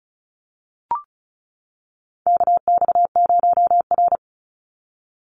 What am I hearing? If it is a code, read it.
Morse code, heard as a keyed tone, about 34 wpm, 700 Hz. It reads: KX0R
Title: NA SOTA Chaser Callsigns 35wpm